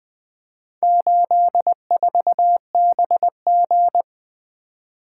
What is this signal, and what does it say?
Morse code, heard as a keyed tone, about 20 wpm, 700 Hz. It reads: 84BG
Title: Four Letter-Number Combinations - 20wpm